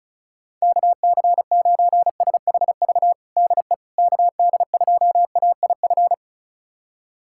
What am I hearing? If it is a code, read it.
Morse code, heard as a keyed tone, about 36 wpm, 700 Hz. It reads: KC9SHV DE KD2AIF
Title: USA Callsign DE Callsign - 6 Characters - 35wpm